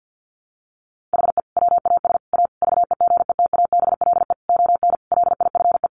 Morse code, heard as a keyed tone, about 50 wpm, 700 Hz. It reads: HE WAS A VEGETABLE ON LIFE
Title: Sentences from Top 1300 Words 50wpm